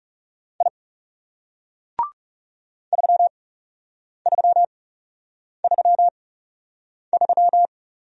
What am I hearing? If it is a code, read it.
Morse code, heard as a keyed tone, about 40 wpm, spaced as for 8 wpm, 700 Hz. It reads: I 3333